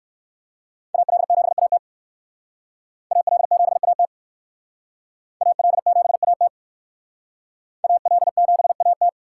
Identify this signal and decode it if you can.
Morse code, heard as a keyed tone, about 58 wpm, 700 Hz. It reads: AL7AT AL7AT AL7AT AL7AT